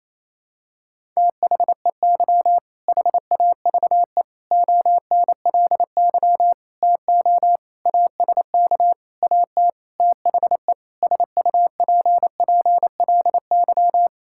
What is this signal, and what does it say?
Morse code, heard as a keyed tone, about 28 wpm, 700 Hz. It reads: THEY HAVE ONLY TO ASK AT THE SUPPLY
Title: Sentences from Top 1000 Words - Review Part 5 - 28wpm